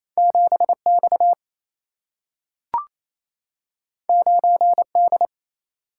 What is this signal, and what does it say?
Morse code, heard as a keyed tone, about 28 wpm, 700 Hz. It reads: 7X 9D
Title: Single-Letter Single-Number 28wpm